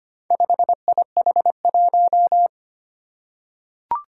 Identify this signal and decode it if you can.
Morse code, heard as a keyed tone, about 25 wpm, 700 Hz. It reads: 5IH1
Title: Four Challenging Letter-Number Combinations 25wpm